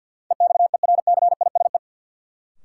Morse code, heard as a keyed tone, about 51 wpm, 700 Hz. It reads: EXERCISE
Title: Sentences from Top 1200 Words - New Word Drill Alt - 50wpm